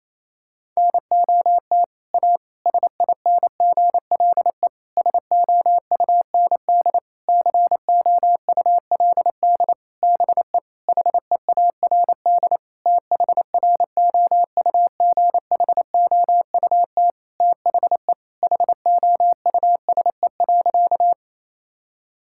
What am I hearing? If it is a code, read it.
Morse code, heard as a keyed tone, about 28 wpm, 700 Hz. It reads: NOT A SINGLE SOUND COULD BE HEARD THROUGHOUT THE HOUSE.
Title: Sentences from Top 1000 Words - Review Part 5 - 28wpm